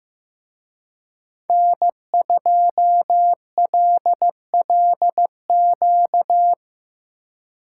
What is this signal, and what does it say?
Morse code, heard as a keyed tone, about 18 wpm, 700 Hz. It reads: N2LLQ
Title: Speed Racing - USA 1x3 Call Signs 15wpm